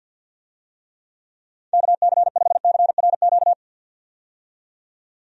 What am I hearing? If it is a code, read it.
Morse code, heard as a keyed tone, about 50 wpm, 700 Hz. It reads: KC5CRQ
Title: USA Call Signs - 6 Characters - 50wpm